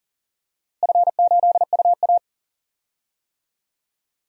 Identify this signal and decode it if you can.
Morse code, heard as a keyed tone, about 40 wpm, 700 Hz. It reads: F8UA
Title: DXCC-100 Call Signs - 4 Characters - Repeated 3x - 40wpm